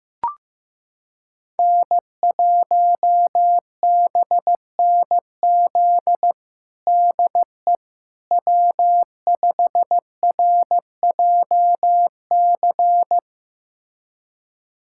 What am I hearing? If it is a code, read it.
Morse code, heard as a keyed tone, about 15 wpm, 700 Hz. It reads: N1BNZ DE W5RJC
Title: USA Callsign DE Callsign - 5 Characters - 15wpm